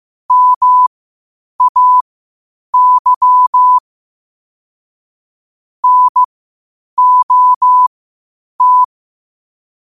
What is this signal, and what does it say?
Morse code, heard as a keyed tone, about 15 wpm, spaced as for 8 wpm, 1000 Hz. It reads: MAY NOT
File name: qotdfiles / 8wpm